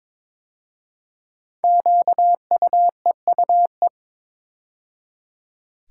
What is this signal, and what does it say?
Morse code, heard as a keyed tone, about 22 wpm, 700 Hz. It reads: QUEUE
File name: 5 Letter Words - First 3 Letters Given - 22wpm